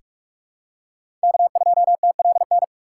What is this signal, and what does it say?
Morse code, heard as a keyed tone, about 45 wpm, 700 Hz. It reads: K2TLN
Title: USA Call Signs - 5 Characters - 6x Character Spacing - 45wpm